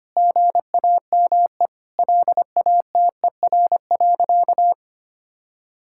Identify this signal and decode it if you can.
Morse code, heard as a keyed tone, about 25 wpm, 700 Hz. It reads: GAME LATER.
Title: Sentences from Top 1000 Words - Review Part 6 - 25wpm